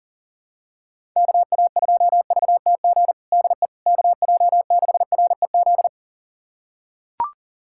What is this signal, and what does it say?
Morse code, heard as a keyed tone, about 40 wpm, 700 Hz. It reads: KA2VTG DE KJ6REZ
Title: USA Callsign DE Callsign - 6 Characters - 40wpm